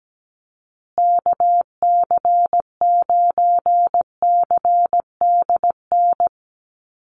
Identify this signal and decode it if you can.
Morse code, heard as a keyed tone, about 17 wpm, 700 Hz. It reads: KC9CDN
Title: USA Call Signs Repeated 2x 17wpm